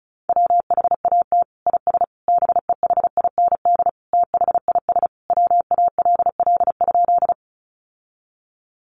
Morse code, heard as a keyed tone, about 34 wpm, 700 Hz. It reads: WHAT IS BEHIND THIS WALL?
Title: Sentences from Top 1100 Words 35wpm